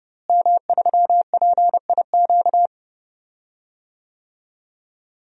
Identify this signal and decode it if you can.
Morse code, heard as a keyed tone, about 31 wpm, 700 Hz. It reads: M3PIQ
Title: DXCC-100 Call Signs - 5 Characters - Repeated 2x - 30wpm